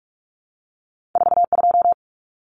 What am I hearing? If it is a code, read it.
Morse code, heard as a keyed tone, about 46 wpm, 700 Hz. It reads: 42
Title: Common Cut Numbers - 2-Digit 45wpm